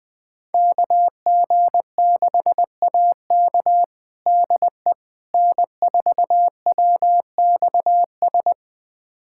Morse code, heard as a keyed tone, about 20 wpm, 700 Hz. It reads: KG6AK DE N4WXS
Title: USA Callsign DE Callsign - 5 Characters - 20wpm